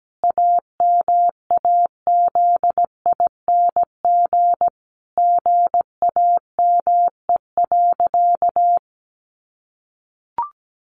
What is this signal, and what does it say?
Morse code, heard as a keyed tone, about 17 wpm, 700 Hz. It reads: AMAZING GAME.
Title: Sentences from Top 1500 Words 17wpm